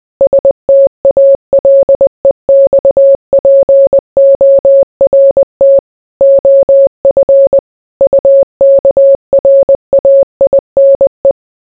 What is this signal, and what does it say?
Morse code, heard as a keyed tone, about 20 wpm, 550 Hz. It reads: STALEXPORT OF UKRAINE